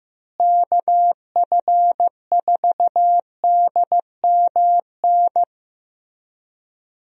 Morse code, heard as a keyed tone, about 15 wpm, 700 Hz. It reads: KF4DMN
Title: USA Call Signs - 6 Characters - Repeated 3x - 15wpm